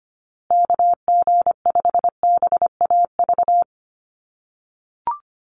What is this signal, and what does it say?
Morse code, heard as a keyed tone, about 25 wpm, 700 Hz. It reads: KG5BAV
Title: USA Call Signs Repeated 2x 25wpm